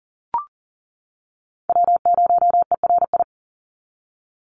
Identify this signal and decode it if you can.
Morse code, heard as a keyed tone, about 40 wpm, 700 Hz. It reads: W0ERI
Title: NA SOTA Chaser Callsigns 40wpm